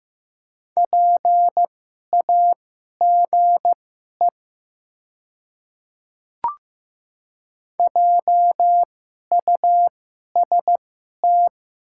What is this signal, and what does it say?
Morse code, heard as a keyed tone, about 15 wpm, spaced as for 11 wpm, 700 Hz. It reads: PAGE JUST
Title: 2x Character Spacing - Top 300 Words - 15wpm